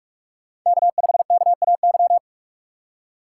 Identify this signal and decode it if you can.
Morse code, heard as a keyed tone, about 41 wpm, 700 Hz. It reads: K5KAY
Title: USA Call Signs - 5 Characters - 3x Character Spacing - 45wpm